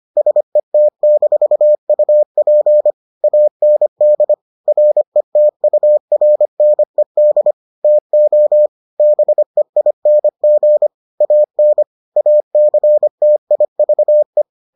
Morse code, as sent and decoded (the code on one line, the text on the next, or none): ... - -....- ..- .--. / .- -. -.. / .-. - ..- .-. -. -.. / - --- / -... .. -. --. / .- -. / .- -.-. - .. ...- .
SET-UP AND RETURNED TO BEING AN ACTIVE